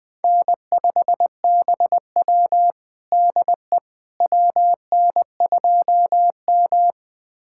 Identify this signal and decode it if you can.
Morse code, heard as a keyed tone, about 20 wpm, 700 Hz. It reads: N5BW DE WN2M
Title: USA Callsign DE Callsign - 4 Characters - 20wpm